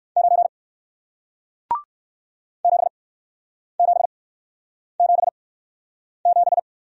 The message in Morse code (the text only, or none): 9 7777